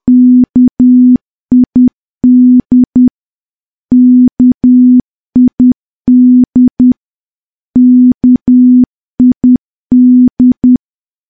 Morse code, heard as a keyed tone, about 10 wpm, 250 Hz. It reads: KID KID KID
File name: morse / morse-training-10WPM